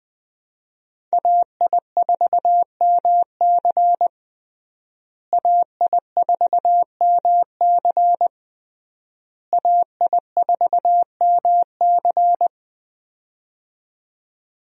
Morse code, heard as a keyed tone, about 20 wpm, 700 Hz. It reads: AI4MC AI4MC AI4MC
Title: USA Call Signs - 5 Characters - Repeated 3x - 20wpm